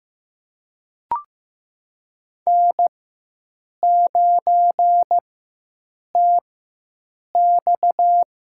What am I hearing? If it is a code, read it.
Morse code, heard as a keyed tone, about 15 wpm, spaced as for 7 wpm, 700 Hz. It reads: N9TX
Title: USA Call Signs - 4x Character Spacing - 15wpm